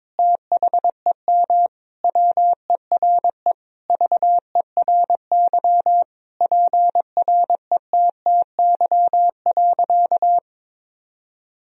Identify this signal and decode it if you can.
Morse code, heard as a keyed tone, about 21 wpm, 700 Hz. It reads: THEM WERE VERY PRETTY.
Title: Sentences from Top 1000 Words - Review Part 1 - 22wpm